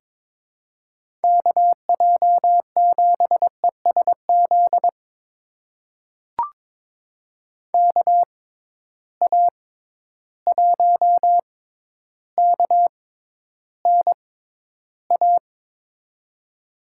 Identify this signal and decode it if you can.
Morse code, heard as a keyed tone, about 22 wpm, spaced as for 11 wpm, 700 Hz. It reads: KJ7ESZ KA1KNA
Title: USA Call Signs - 6 Characters - 6x Character Spacing - 22wpm